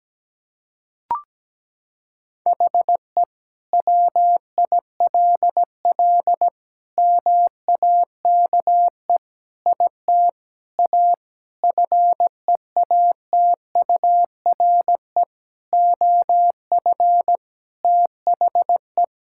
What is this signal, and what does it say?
Morse code, heard as a keyed tone, about 17 wpm, 700 Hz. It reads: HE WILL MAKE IT A FEATURE OF THE
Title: Sentences from Top 1000 Words - Review Part 5 - 17wpm